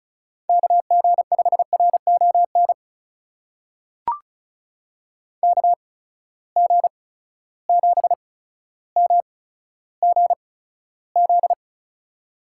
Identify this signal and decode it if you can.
Morse code, heard as a keyed tone, about 36 wpm, spaced as for 15 wpm, 700 Hz. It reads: KG5RON KG7MGZ
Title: USA Call Signs - 6 Characters - 8x Character Spacing - 35wpm